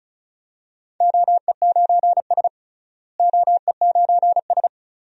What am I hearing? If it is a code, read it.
Morse code, heard as a keyed tone, about 36 wpm, 700 Hz. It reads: OE9S OE9S
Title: DXCC-100 Call Signs - 4 Characters - Repeated 2x - 35wpm